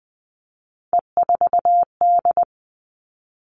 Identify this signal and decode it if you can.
Morse code, heard as a keyed tone, about 20 wpm, 700 Hz. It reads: E4D
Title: All Cut Numbers - 3-Digit 20wpm